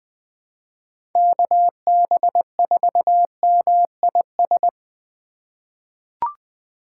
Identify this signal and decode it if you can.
Morse code, heard as a keyed tone, about 20 wpm, 700 Hz. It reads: KB4MIS
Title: USA Call Signs - 6 Characters - 2x Character Spacing - 20wpm